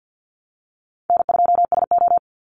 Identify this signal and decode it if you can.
Morse code, heard as a keyed tone, about 50 wpm, 700 Hz. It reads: N2SO
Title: Easy Call Signs Repeated 2x - 50wpm